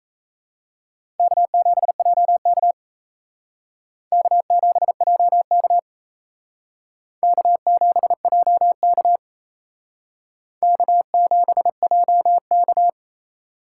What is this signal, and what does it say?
Morse code, heard as a keyed tone, about 36 wpm, 700 Hz. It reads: K7JK K7JK K7JK K7JK